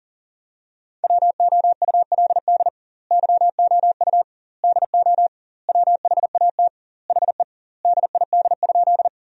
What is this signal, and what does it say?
Morse code, heard as a keyed tone, about 40 wpm, 700 Hz. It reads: WOULD YOU DO WHAT HE DID?
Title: 2x Word-Spacing - Sentences from Top 100 Words - 40wpm